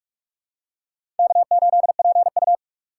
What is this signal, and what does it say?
Morse code, heard as a keyed tone, about 46 wpm, 700 Hz. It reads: K8PU
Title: USA Call Signs - 4 Characters - 4x Character Spacing - 45wpm